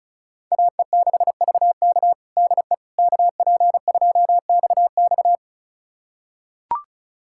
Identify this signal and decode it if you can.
Morse code, heard as a keyed tone, about 36 wpm, 700 Hz. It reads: AE6VK DE KP2XX
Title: USA Callsign DE Callsign - 5 Characters - 35wpm